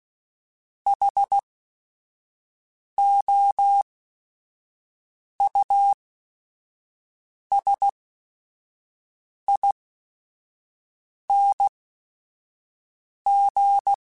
Morse code, heard as a keyed tone, about 16 wpm, spaced as for 5 wpm, 800 Hz. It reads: HOUSING
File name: Special INGs / Special INGs-mp3s / Sngl Ing @5